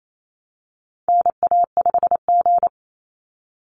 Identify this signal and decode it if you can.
Morse code, heard as a keyed tone, about 28 wpm, 700 Hz. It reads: NA5G
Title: Call Signs Repeated 3x - 28wpm